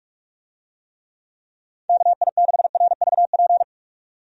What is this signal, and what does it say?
Morse code, heard as a keyed tone, about 46 wpm, 700 Hz. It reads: KI6RVP
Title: USA Call Signs - 6 Characters - 45wpm